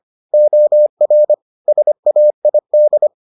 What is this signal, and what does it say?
Morse code, heard as a keyed tone, about 25 wpm, 600 Hz. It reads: OR SAID